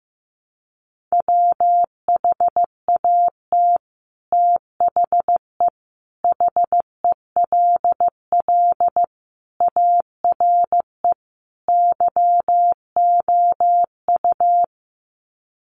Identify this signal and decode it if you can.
Morse code, heard as a keyed tone, about 15 wpm, 700 Hz. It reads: WHAT THE HELL ARE YOU
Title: Sets of 5 Words Encore 15wpm